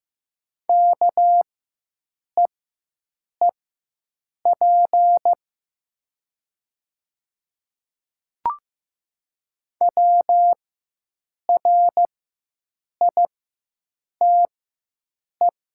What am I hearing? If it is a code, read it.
Morse code, heard as a keyed tone, about 15 wpm, spaced as for 7 wpm, 700 Hz. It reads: KEEP WRITE
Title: 4x Character Spacing - Top 500 Words - 15wpm